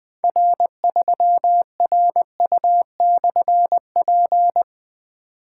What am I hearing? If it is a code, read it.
Morse code, heard as a keyed tone, about 20 wpm, 700 Hz. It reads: R3RU/P
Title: Difficult Call Signs 20wpm